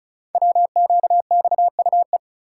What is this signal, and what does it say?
Morse code, heard as a keyed tone, about 36 wpm, 700 Hz. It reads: WQXUE
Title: Code Groups - 5 Letters - 35wpm